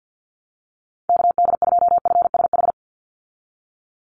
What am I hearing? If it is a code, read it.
Morse code, heard as a keyed tone, about 49 wpm, 700 Hz. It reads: KD2FSH
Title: Call Signs Repeated 2x - 50wpm